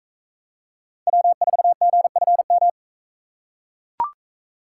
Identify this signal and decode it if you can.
Morse code, heard as a keyed tone, about 40 wpm, 700 Hz. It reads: W4GFM